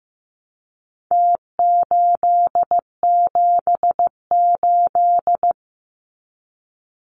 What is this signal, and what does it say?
Morse code, heard as a keyed tone, about 15 wpm, 700 Hz. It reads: T878
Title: Numbers - Contest Style - 4-Digit 15wpm